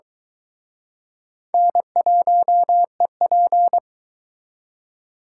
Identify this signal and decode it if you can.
Morse code, heard as a keyed tone, about 23 wpm, 700 Hz. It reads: N1EP